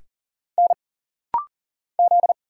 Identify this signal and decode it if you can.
Morse code, heard as a keyed tone, about 39 wpm, 700 Hz. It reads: N Z
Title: Single Letter-Number - Rapid-Fire 40wpm